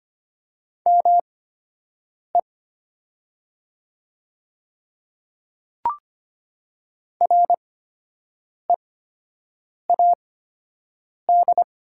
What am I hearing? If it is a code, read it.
Morse code, heard as a keyed tone, about 25 wpm, spaced as for 7 wpm, 700 Hz. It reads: ME READ
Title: 8x Character Spacing - Top 300 Words - 25wpm